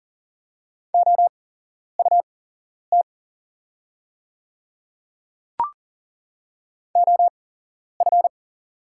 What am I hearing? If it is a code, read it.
Morse code, heard as a keyed tone, about 40 wpm, spaced as for 11 wpm, 690 Hz. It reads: OUT OF
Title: 8x Character Spacing - Top 300 Words - 40wpm